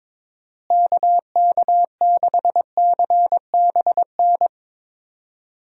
Morse code, heard as a keyed tone, about 22 wpm, 700 Hz. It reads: KK6CBN
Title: USA Call Signs - 6 Characters - Repeated 2x - 22wpm